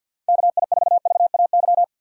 Morse code, heard as a keyed tone, about 52 wpm, 700 Hz. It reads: KI4VAY
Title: USA Call Signs - 6x Character Spacing - 50wpm